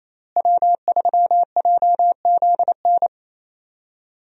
Morse code, heard as a keyed tone, about 28 wpm, 700 Hz. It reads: W3JZN
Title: USA Call Signs - 5 Characters - Repeated 2x - 28wpm